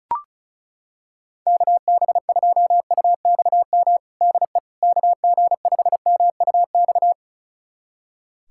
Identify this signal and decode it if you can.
Morse code, heard as a keyed tone, about 36 wpm, 700 Hz. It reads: KB2UXM DE KG5MUX
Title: USA Callsign DE Callsign - 6 Characters - 35wpm